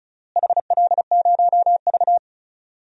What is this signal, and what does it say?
Morse code, heard as a keyed tone, about 36 wpm, 700 Hz. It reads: HL0V
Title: DXCC-100 Call Signs - 4 Characters - Repeated 3x - 35wpm